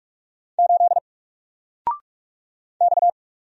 Morse code, heard as a keyed tone, about 44 wpm, 700 Hz. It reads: 8 X
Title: Single Character (Letter:Number:Punctuation:Prosign) Alt 45wpm